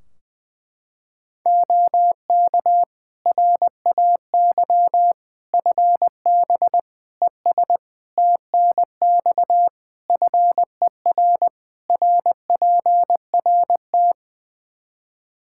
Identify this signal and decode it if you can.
Morse code, heard as a keyed tone, about 21 wpm, 700 Hz. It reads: OK RAY FB ES TNX FER RPRT
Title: Practice QSO - Exchange 2 - Info - 20wpm